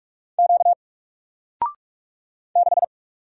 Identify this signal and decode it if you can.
Morse code, heard as a keyed tone, about 46 wpm, 700 Hz. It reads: Q 6
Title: Single Character (Letter:Number:Punctuation:Prosign) 45wpm